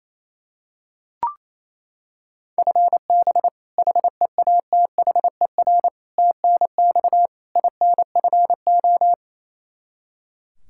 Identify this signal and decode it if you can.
Morse code, heard as a keyed tone, about 28 wpm, 700 Hz. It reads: FB HEATHER TNX INFO
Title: Practice QSO - Exchange 3 - More Info - 28wpm